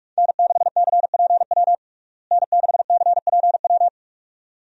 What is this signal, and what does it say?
Morse code, heard as a keyed tone, about 41 wpm, 700 Hz. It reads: N6CPW N6CPW
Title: USA Call Signs - 5 Characters - Repeated 2x - 45wpm